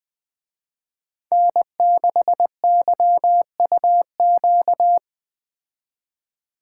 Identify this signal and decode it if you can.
Morse code, heard as a keyed tone, about 20 wpm, 700 Hz. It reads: N6YUQ
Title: Speed Racing - USA 1x3 Call Signs 17wpm